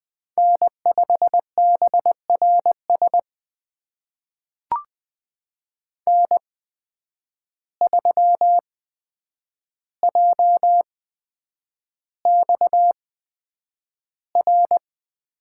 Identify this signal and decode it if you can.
Morse code, heard as a keyed tone, about 20 wpm, spaced as for 9 wpm, 700 Hz. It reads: N5BRS N3JXR
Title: USA Call Signs - 5 Characters - 8x Character Spacing - 20wpm